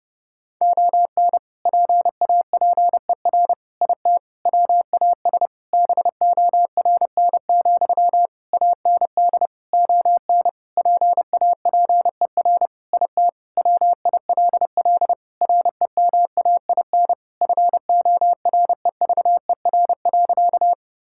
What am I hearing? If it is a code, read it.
Morse code, heard as a keyed tone, about 30 wpm, 700 Hz. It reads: ON PAPER IT WAS BORN, AND ON PAPER IT WILL REMAIN FOREVER.
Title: Sentences from Top 1600 Words 30wpm